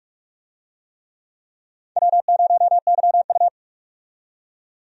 Morse code, heard as a keyed tone, about 46 wpm, 700 Hz. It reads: W0YU
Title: USA Call Signs - 4 Characters - 45wpm